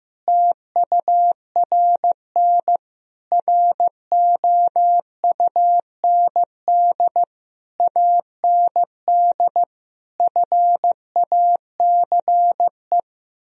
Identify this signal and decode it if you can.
Morse code, heard as a keyed tone, about 15 wpm, 700 Hz. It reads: TURN ROUND AND FACE
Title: Sentences from Top 1000 Words - Review Part 2 - 15wpm